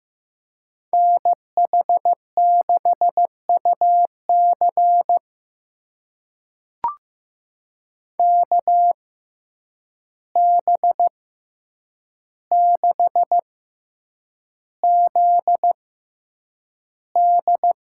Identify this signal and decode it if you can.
Morse code, heard as a keyed tone, about 15 wpm, spaced as for 8 wpm, 700 Hz. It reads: NH6UC KB6ZD
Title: USA Call Signs - 5 Characters - 6x Character Spacing - 15wpm